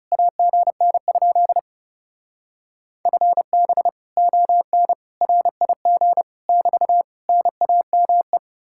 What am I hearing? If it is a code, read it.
Morse code, heard as a keyed tone, about 37 wpm, 700 Hz. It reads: AGN? FB ON RIG <BT> NAME
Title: Speed-Racing Common QSO Phrases - 30wpm